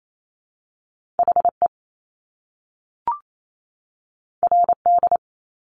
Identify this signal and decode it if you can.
Morse code, heard as a keyed tone, about 28 wpm, 700 Hz. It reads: H E R D
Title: Top Two Letter Combinations in Words 28wpm